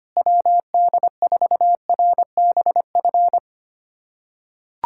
Morse code, heard as a keyed tone, about 25 wpm, 700 Hz. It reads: WD4RBF
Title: USA Callsign DE Callsign - 6 Characters - 25wpm